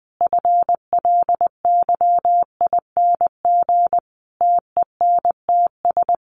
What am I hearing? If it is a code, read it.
Morse code, heard as a keyed tone, about 20 wpm, 700 Hz. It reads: FLYING TENTS
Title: A Sound Of Thunder 20wpm